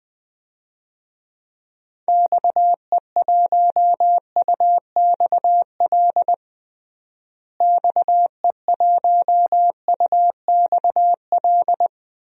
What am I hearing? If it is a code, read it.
Morse code, heard as a keyed tone, about 20 wpm, 700 Hz. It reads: XE1UXL XE1UXL
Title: DXCC-100 Call Signs - 6 Characters - Repeated 2x - 20wpm